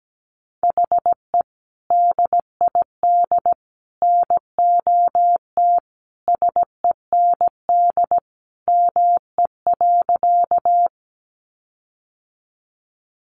Morse code, heard as a keyed tone, about 17 wpm, 700 Hz. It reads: HE DID NOT SEND ME.
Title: Sentences from Top 1500 Words 17wpm